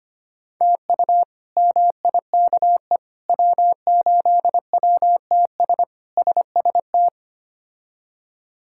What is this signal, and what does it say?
Morse code, heard as a keyed tone, about 25 wpm, 700 Hz. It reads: TU MIKE W8WTS SST
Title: K1USN SST QSO Practice - 25wpm